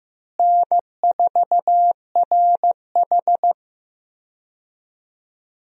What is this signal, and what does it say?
Morse code, heard as a keyed tone, about 15 wpm, 700 Hz. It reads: N4RH
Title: USA Call Signs - 4 Characters - Repeated 2x - 15wpm